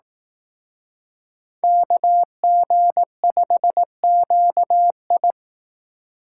Morse code, heard as a keyed tone, about 18 wpm, 700 Hz. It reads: KG5QI
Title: Speed Racing - USA 2x2 Call Signs 15wpm